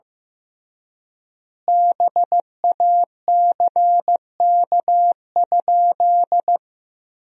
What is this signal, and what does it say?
Morse code, heard as a keyed tone, about 15 wpm, 700 Hz. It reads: BACK?
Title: 3x Word-Spacing - Sentences from Top 300 Words - 15wpm